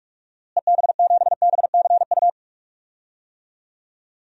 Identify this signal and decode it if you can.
Morse code, heard as a keyed tone, about 46 wpm, 700 Hz. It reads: EB7BCU
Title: DXCC-100 Call Signs - 6 Characters - Repeated 3x - 45wpm